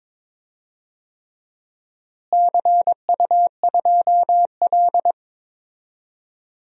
Morse code, heard as a keyed tone, about 22 wpm, 700 Hz. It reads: CU2L
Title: DXCC-100 Call Signs - 4 Characters - Repeated 3x - 22wpm